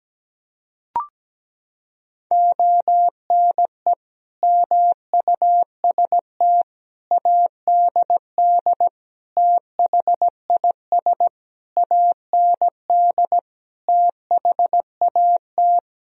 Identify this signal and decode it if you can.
Morse code, heard as a keyed tone, about 17 wpm, 700 Hz. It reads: ONE MUST ADD THIS AND THAT
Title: Sentences from Top 1000 Words - Review Part 1 - 17wpm